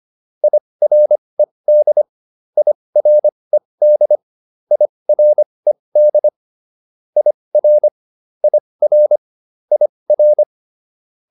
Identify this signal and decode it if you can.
Morse code, heard as a keyed tone, about 25 wpm, spaced as for 20 wpm, 600 Hz. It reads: IRED IRED IRED IR IR IR